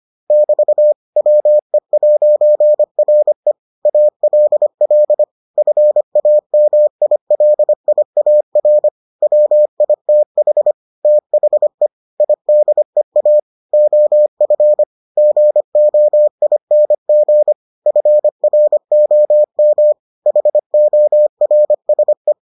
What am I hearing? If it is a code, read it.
Morse code, heard as a keyed tone, about 25 wpm, 600 Hz. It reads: <BT> WE'RE ALL FAMILIAR WITH THE IDEA OF GOING FROM HORSE